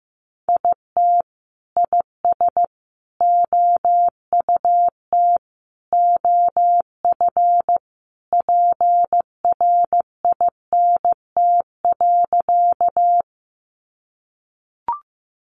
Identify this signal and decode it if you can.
Morse code, heard as a keyed tone, about 14 wpm, 700 Hz. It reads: IT IS OUT OF PRINT.
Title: Sentences from Top 1400 Words 15wpm